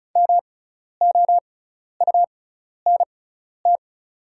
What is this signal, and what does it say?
Morse code, heard as a keyed tone, about 36 wpm, spaced as for 12 wpm, 700 Hz. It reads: MOUNT
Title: 6x Character Spacing - Top 500 Words - 35wpm